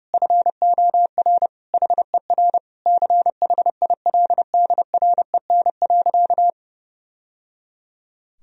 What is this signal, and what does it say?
Morse code, heard as a keyed tone, about 29 wpm, 700 Hz. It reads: FOR HER CHILDREN.
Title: Sentences from Top 1000 Words - Review Part 3 - 30wpm